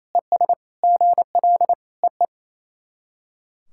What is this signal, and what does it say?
Morse code, heard as a keyed tone, about 28 wpm, 700 Hz. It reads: ES GL EE
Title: SOTA QSO Practice 28wpm